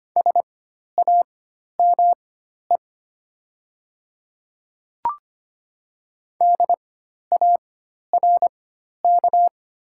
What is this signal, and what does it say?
Morse code, heard as a keyed tone, about 25 wpm, spaced as for 12 wpm, 690 Hz. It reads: SAME DARK
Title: 4x Character Spacing - Top 500 Words - 25wpm